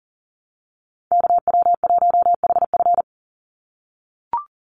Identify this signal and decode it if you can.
Morse code, heard as a keyed tone, about 40 wpm, 700 Hz. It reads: KW1HF
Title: USA Call Signs 40wpm